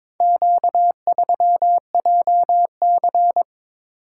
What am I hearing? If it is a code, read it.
Morse code, heard as a keyed tone, about 22 wpm, 700 Hz. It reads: Q3JC
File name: practice / 22wpm / Four Letter-Number Combinations - 22wpm